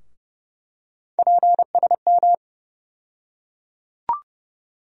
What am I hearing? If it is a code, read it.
Morse code, heard as a keyed tone, about 30 wpm, 700 Hz. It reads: PSM
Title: USA Call Sign Postfixes - 3 Letters 30wpm